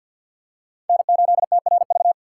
Morse code, heard as a keyed tone, about 48 wpm, 700 Hz. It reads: N8TRV
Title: USA Call Signs - 5 Characters - 3x Character Spacing - 50wpm